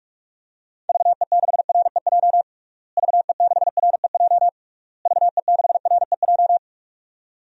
Morse code, heard as a keyed tone, about 46 wpm, 700 Hz. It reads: VE6REJ VE6REJ VE6REJ
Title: DXCC-100 Call Signs - 6 Characters - Repeated 3x - 45wpm